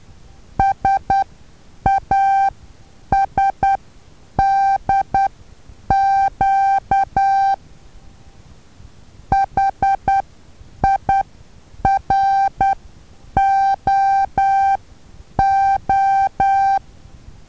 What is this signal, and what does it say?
Morse code, heard as a keyed tone, about 10 wpm, spaced as for 8 wpm, 780 Hz. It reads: SASDQ HIROO
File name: Taviro HA6VA A